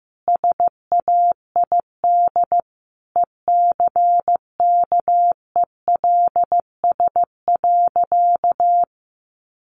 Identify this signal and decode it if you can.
Morse code, heard as a keyed tone, about 14 wpm, 700 Hz. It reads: SAID ECKELS.
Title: A Sound Of Thunder 15wpm